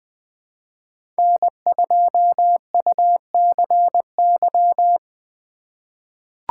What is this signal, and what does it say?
Morse code, heard as a keyed tone, about 20 wpm, 700 Hz. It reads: N2UCY
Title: USA Call Signs - 5 Characters - 2x Character Spacing - 20wpm